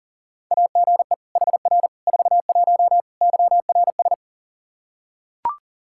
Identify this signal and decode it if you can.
Morse code, heard as a keyed tone, about 40 wpm, 700 Hz. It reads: AGE HR 41 YRS
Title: Practice QSO - Exchange 3 - More Info - 40wpm